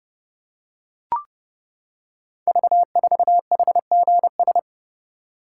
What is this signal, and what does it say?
Morse code, heard as a keyed tone, about 30 wpm, 690 Hz. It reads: V4HGS
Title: Code Groups - 5 Letter:Numbers - 30wpm